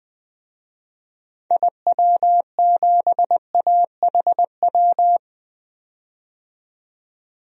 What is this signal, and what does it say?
Morse code, heard as a keyed tone, about 20 wpm, 700 Hz. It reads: IW7AHW
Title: DXCC-100 Call Signs - 6 Characters - 20wpm